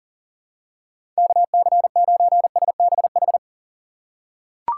Morse code, heard as a keyed tone, about 40 wpm, 700 Hz. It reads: KC9SBH
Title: USA Call Signs - 6 Characters - 4x Character Spacing - 40wpm